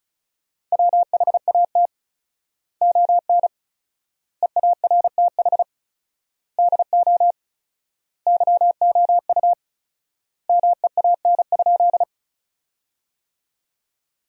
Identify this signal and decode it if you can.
Morse code, heard as a keyed tone, about 36 wpm, 700 Hz. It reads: WHAT ON EARTH DO YOU MEAN?
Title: 4x Word-Spacing - Sentences from Top 200 Words - 35wpm